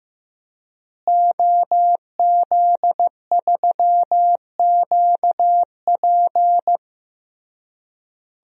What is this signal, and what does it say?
Morse code, heard as a keyed tone, about 15 wpm, 700 Hz. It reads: OZ3QP
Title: DXCC-100 Call Signs - 5 Characters - Repeated 3x - 15wpm